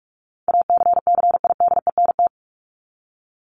Morse code, heard as a keyed tone, about 44 wpm, 700 Hz. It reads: ACCIDENT
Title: Top 1751-2000 Words 45wpm